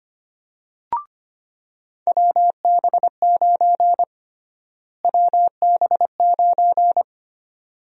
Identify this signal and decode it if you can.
Morse code, heard as a keyed tone, about 25 wpm, 700 Hz. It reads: WB9 WB9
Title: USA Call Sign Prefixes - Repeated 2x - 25wpm